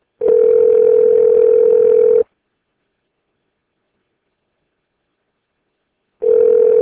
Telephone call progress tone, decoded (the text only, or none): ringback tone